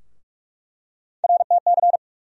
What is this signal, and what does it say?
Morse code, heard as a keyed tone, about 46 wpm, 700 Hz. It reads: RTC
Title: USA Call Sign Postfixes - 3 Letters 45wpm